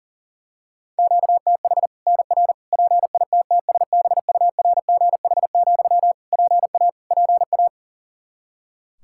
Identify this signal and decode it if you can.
Morse code, heard as a keyed tone, about 37 wpm, 700 Hz. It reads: QTH NR PITTSBURGH, PA PA
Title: Practice QSO - Exchange 1 - Beginning - 40wpm